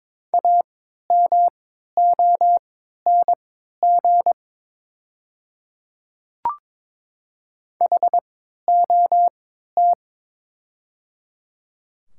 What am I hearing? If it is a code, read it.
Morse code, heard as a keyed tone, about 22 wpm, spaced as for 13 wpm, 700 Hz. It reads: AMONG HOT
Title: 3x Character Spacing - Top 500 Words - 22wpm